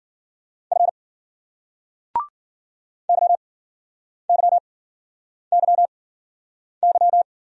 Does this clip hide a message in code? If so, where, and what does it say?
Yes: Morse code from 0.7 to 7.2 s, heard as a keyed tone, about 54 wpm, 700 Hz. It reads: F Y Y Y Y